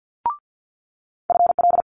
Morse code, heard as a keyed tone, about 50 wpm, 700 Hz. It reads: FL
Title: Multi-letter Phonemes Quick Fire 50wpm